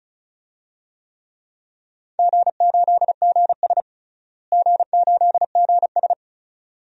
Morse code, heard as a keyed tone, about 34 wpm, 700 Hz. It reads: G8GS G8GS
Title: DXCC-100 Call Signs - 4 Characters - Repeated 2x - 35wpm